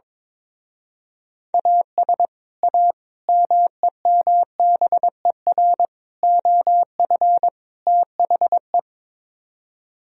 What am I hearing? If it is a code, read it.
Morse code, heard as a keyed tone, about 22 wpm, 700 Hz. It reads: AS A MEMBER OF THE
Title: Speed-Racing - Sets of 5 Words 22wpm